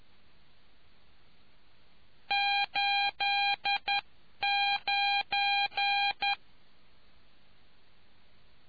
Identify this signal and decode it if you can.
Morse code, heard as a keyed tone, about 11 wpm, 800 Hz. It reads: 89